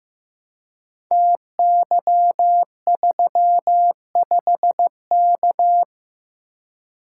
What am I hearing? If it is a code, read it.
Morse code, heard as a keyed tone, about 15 wpm, 700 Hz. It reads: TY35K
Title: Code Groups - 5 Letter:Numbers - 15wpm